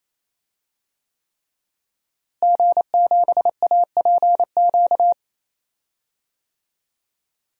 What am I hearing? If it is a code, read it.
Morse code, heard as a keyed tone, about 28 wpm, 700 Hz. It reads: G7APQ
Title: DXCC-100 Call Signs - 5 Characters - Repeated 3x - 28wpm